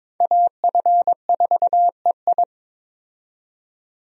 Morse code, heard as a keyed tone, about 22 wpm, 700 Hz. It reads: AF4EI
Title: USA Call Signs - 5 Characters - 22wpm